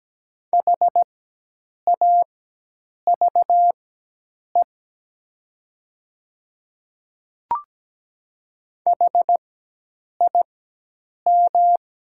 Morse code, heard as a keyed tone, about 17 wpm, spaced as for 8 wpm, 700 Hz. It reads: HAVE HIM
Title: 4x Character Spacing - Top 100 Words - 17wpm